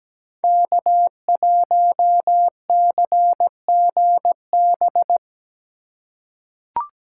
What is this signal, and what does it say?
Morse code, heard as a keyed tone, about 17 wpm, 700 Hz. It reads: K1CGB
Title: Code Groups - 5 Letter:Numbers - 17wpm